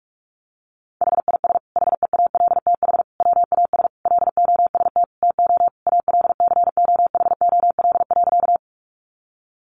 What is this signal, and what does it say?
Morse code, heard as a keyed tone, about 44 wpm, 700 Hz. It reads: HIS HEALTH WAS LOST TO ALCOHOL.
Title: Sentences from Top 1400 Words 45wpm